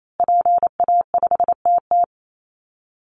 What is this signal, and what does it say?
Morse code, heard as a keyed tone, about 28 wpm, 700 Hz. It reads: PA5TT
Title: Call Signs Repeated 2x - 28wpm